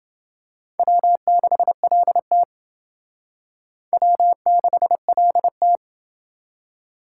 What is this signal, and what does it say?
Morse code, heard as a keyed tone, about 28 wpm, 700 Hz. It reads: W6LT W6LT